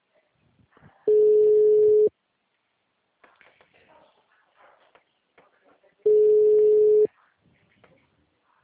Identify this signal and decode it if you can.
Ringback tone (Europe, 425 Hz), with speech.
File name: out- -2002- - - .247